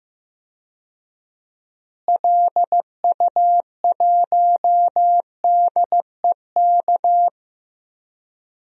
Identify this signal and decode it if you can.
Morse code, heard as a keyed tone, about 15 wpm, 700 Hz. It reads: LU1DEK
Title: DXCC-100 Call Signs - 6 Characters - Repeated 3x - 15wpm